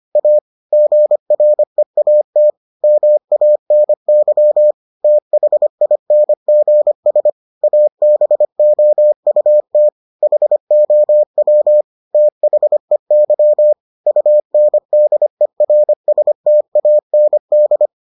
Morse code, heard as a keyed tone, about 25 wpm, 600 Hz. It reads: A GREAT MANY THINGS ABOUT HOW THEY UNDERSTAND